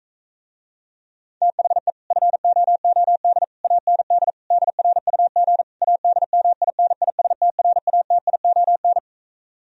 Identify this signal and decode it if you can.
Morse code, heard as a keyed tone, about 44 wpm, 700 Hz. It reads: THE FOOD AND DRUG ADMINISTRATION